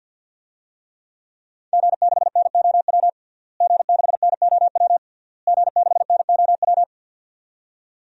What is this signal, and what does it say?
Morse code, heard as a keyed tone, about 52 wpm, 700 Hz. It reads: G6NOW G6NOW G6NOW
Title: DXCC-100 Call Signs - 5 Characters - Repeated 3x - 50wpm